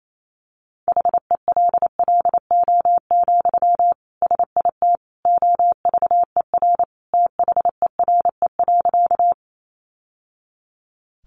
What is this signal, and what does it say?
Morse code, heard as a keyed tone, about 28 wpm, 700 Hz. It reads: HELLO, SIT OVER THERE.
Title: Sentences from Top 1100 Words 28wpm